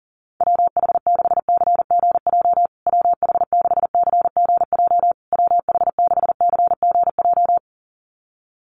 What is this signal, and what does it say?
Morse code, heard as a keyed tone, about 40 wpm, 700 Hz. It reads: WH6CGJ WH6CGJ WH6CGJ
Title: USA Call Signs Repeated 3x 40wpm